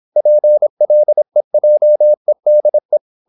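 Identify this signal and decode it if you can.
Morse code, heard as a keyed tone, about 26 wpm, 600 Hz. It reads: PLEJEDE
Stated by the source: High-speed (QRQ) Morse at 26 wpm.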